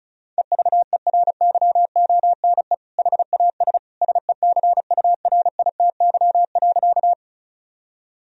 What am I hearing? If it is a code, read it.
Morse code, heard as a keyed tone, about 34 wpm, 700 Hz. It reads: EVERYONE HAS SECURITY.
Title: Sentences from Top 1000 Words - Review Part 4 - 35wpm